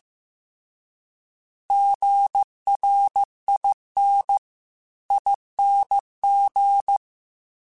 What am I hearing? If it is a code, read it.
Morse code, heard as a keyed tone, about 15 wpm, 800 Hz. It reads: GRINING
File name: Alphabetical / Alphabetical mp3s / Gs @13